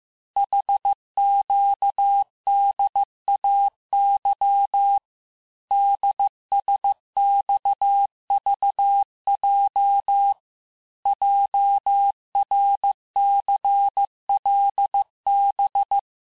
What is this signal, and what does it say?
Morse code, heard as a keyed tone, about 15 wpm, 800 Hz. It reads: HQDAY DSXVJ JRCLB
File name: cours-du-lundi files / cours-1585-texte-chiffre-vitesse-14mpm